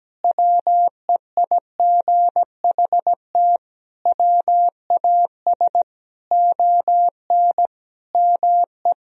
Morse code, heard as a keyed tone, about 17 wpm, 700 Hz. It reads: WEIGHT WAS ON ME
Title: Sentences from Top 1000 Words - Review Part 3 - 17wpm